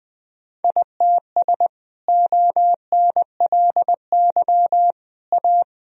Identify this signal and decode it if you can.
Morse code, heard as a keyed tone, about 20 wpm, 700 Hz. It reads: ITS ONLY A
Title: Sentences from Top 1000 Words - Review Part 1 - 20wpm